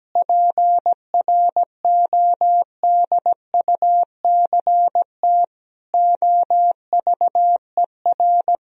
Morse code, heard as a keyed tone, about 17 wpm, 700 Hz. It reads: PRODUCT OVER
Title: Sentences from Top 1000 Words - Review Part 2 - 17wpm